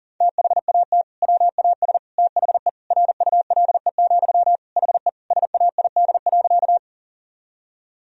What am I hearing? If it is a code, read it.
Morse code, heard as a keyed tone, about 40 wpm, 700 Hz. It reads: THAT WAS THE RULE, HE SAID.
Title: Sentences from Top 1000 Words - Review Part 2 - 40wpm